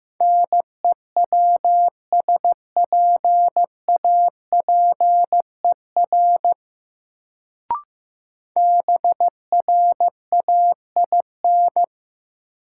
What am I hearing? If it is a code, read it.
Morse code, heard as a keyed tone, about 15 wpm, 700 Hz. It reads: NEWSPAPER BRAIN
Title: Sentences from Top 1100 Words - New Word Drill Alt - 15wpm